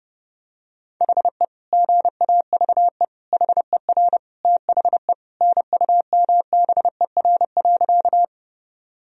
Morse code, heard as a keyed tone, about 30 wpm, 700 Hz. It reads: HE GAVE HER THE NUMBER.
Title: Sentences from Top 1000 Words - Review Part 1 - 30wpm